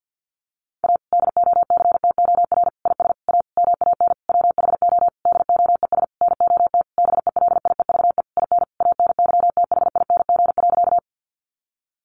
Morse code, heard as a keyed tone, about 50 wpm, 700 Hz. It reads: A DOCTOR IS A MAN WHO DOES NOT BELIEVE IN ANYTHING.
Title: Sentences from Top 1500 Words 50wpm